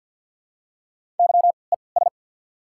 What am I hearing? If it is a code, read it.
Morse code, heard as a keyed tone, about 48 wpm, spaced as for 28 wpm, 700 Hz. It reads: YES
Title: 3x Character Spacing - Top 500 Words - 50wpm